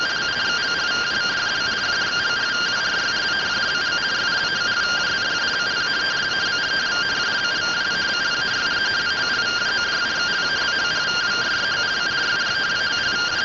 Two-tone teleprinter signal, about 170 Hz shift.